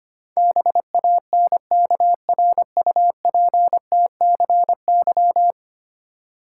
Morse code, heard as a keyed tone, about 25 wpm, 700 Hz. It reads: BANKRUPTCY
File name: Ten Letter Words - 25wpm